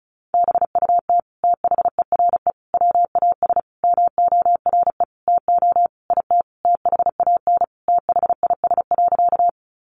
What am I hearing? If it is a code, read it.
Morse code, heard as a keyed tone, about 34 wpm, 700 Hz. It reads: BUT THERE WAS MORE TO IT THAN THIS.
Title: Sentences from Top 100 Words 35wpm